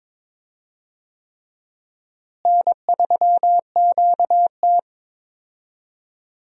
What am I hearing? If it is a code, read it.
Morse code, heard as a keyed tone, about 22 wpm, 700 Hz. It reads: N3QT
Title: USA Call Signs - 4 Characters - Repeated 3x - 22wpm